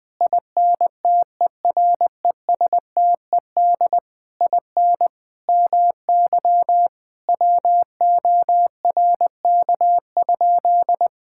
Morse code, heard as a keyed tone, about 20 wpm, 700 Hz. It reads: INTERESTED IN MY WORK?
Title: Sentences from Top 1000 Words - Review Part 6 - 20wpm